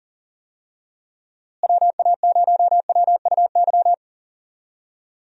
Morse code, heard as a keyed tone, about 40 wpm, 700 Hz. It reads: WA0WUY
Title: USA Call Signs - 6 Characters - Repeated 3x - 40wpm